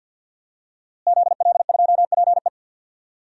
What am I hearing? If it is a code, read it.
Morse code, heard as a keyed tone, about 46 wpm, 700 Hz. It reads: ZL2PE
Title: DXCC-100 Call Signs - 5 Characters - Repeated 3x - 50wpm